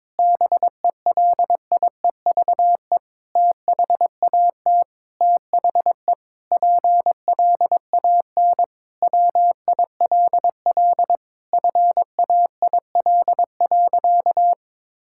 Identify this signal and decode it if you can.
Morse code, heard as a keyed tone, about 22 wpm, 700 Hz. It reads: BELIEVE THAT THE PLAN WILL FAIL.
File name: Sentences with 7-Word Phrases - 22wpm